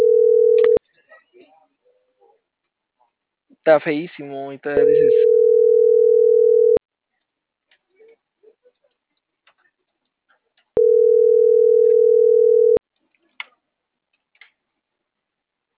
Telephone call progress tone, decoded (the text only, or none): ringback tone